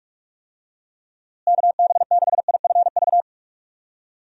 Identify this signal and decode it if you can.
Morse code, heard as a keyed tone, about 42 wpm, 700 Hz. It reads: KB6IFV
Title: USA Call Signs - 6 Characters - Repeated 3x - 45wpm